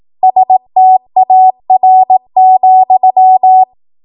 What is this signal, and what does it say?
Morse code, heard as a keyed tone, about 18 wpm, 750 Hz. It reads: STAR,
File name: Morse / 18WPM